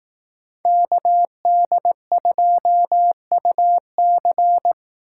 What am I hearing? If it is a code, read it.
Morse code, heard as a keyed tone, about 18 wpm, 700 Hz. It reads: KD2UC
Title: Speed Racing - USA Call Signs 15wpm